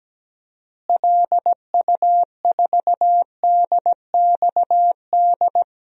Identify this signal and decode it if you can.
Morse code, heard as a keyed tone, about 17 wpm, 700 Hz. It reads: LU4DXD
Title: DXCC-100 Call Signs - 6 Characters - Repeated 3x - 17wpm